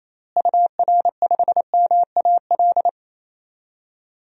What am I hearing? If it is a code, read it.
Morse code, heard as a keyed tone, about 28 wpm, 700 Hz. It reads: UR5MAL
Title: DXCC-100 Call Signs - 6 Characters - Repeated 3x - 28wpm